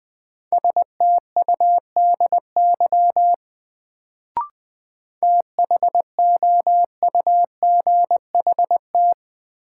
Morse code, heard as a keyed tone, about 20 wpm, 700 Hz. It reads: STUDY THOUGHT
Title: Sentences from Top 300 Words - New Word Drill - 20wpm